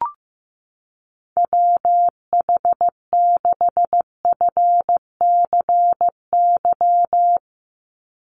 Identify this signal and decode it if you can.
Morse code, heard as a keyed tone, about 15 wpm, 700 Hz. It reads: WH6FCY
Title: USA Call Signs 15wpm